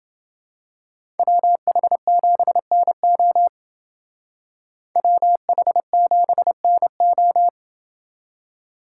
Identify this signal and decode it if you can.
Morse code, heard as a keyed tone, about 29 wpm, 700 Hz. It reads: WH7NO WH7NO